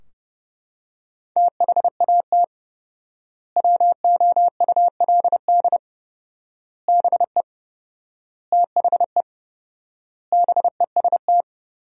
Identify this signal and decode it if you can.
Morse code, heard as a keyed tone, about 30 wpm, 700 Hz. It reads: THAT WOULD BE THE BEST
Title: 4x Word-Spacing - Sentences from Top 400 Words - 30wpm